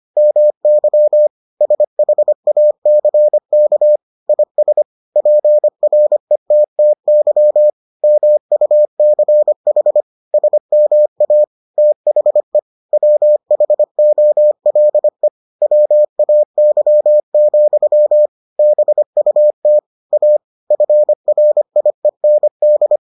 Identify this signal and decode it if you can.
Morse code, heard as a keyed tone, about 25 wpm, 600 Hz. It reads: MY SHACK IS PRETTY MUCH SMA THE WHOLE WAY, BUT A FRIEND